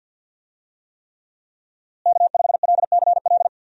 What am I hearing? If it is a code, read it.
Morse code, heard as a keyed tone, about 52 wpm, 700 Hz. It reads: K5LCL
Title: USA Call Signs - 5 Characters - Repeated 3x - 50wpm